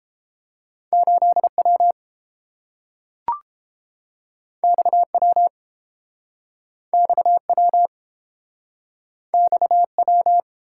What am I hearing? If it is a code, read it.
Morse code, heard as a keyed tone, about 30 wpm, 700 Hz. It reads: 8W XW XW XW